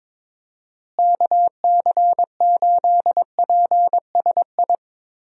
Morse code, heard as a keyed tone, about 22 wpm, 700 Hz. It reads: KC8PSI
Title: USA Call Signs - 6 Characters - Repeated 3x - 22wpm